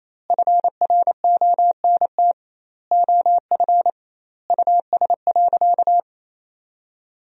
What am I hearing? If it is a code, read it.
Morse code, heard as a keyed tone, about 28 wpm, 700 Hz. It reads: FRONT OF US.
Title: 2x Word-Spacing - Sentences from Top 500 Words - 28wpm